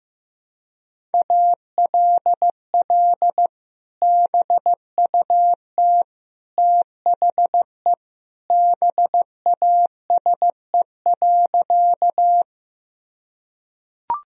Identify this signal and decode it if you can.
Morse code, heard as a keyed tone, about 14 wpm, 700 Hz. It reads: ALL BUT THE BASE.
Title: Sentences from Top 1000 Words - Review Part 1 - 15wpm